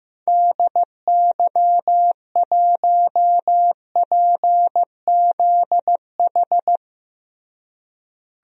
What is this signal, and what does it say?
Morse code, heard as a keyed tone, about 15 wpm, 700 Hz. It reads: DY1PZH
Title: DXCC-100 Call Signs - 6 Characters - Repeated 3x - 15wpm